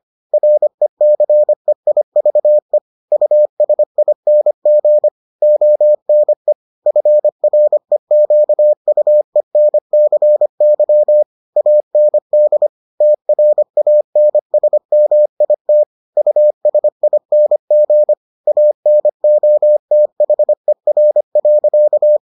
Morse code, heard as a keyed tone, about 25 wpm, 600 Hz. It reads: RECEIVE USING ONE FREQUENCY AND TRANSMIT USING ANOTHER.